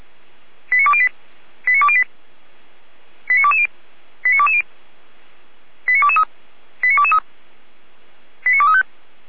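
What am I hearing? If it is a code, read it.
Five-tone selective call (ZVEI), heard as quick runs of five tones, tones of about 70 ms. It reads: 89208 89208 89220 89220 89223 89223 89235
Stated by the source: Five-tone ZVEI selective call received by SDR.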